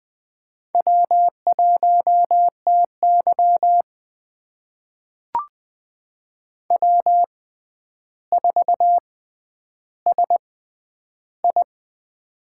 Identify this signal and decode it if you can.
Morse code, heard as a keyed tone, about 20 wpm, spaced as for 10 wpm, 700 Hz. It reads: W1TY W4SI
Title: USA Call Signs - 4 Characters - 6x Character Spacing - 20wpm